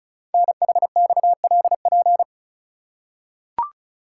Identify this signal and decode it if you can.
Morse code, heard as a keyed tone, about 36 wpm, 700 Hz. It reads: NHXLP
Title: Code Groups - 5 Letter:Numbers - 35wpm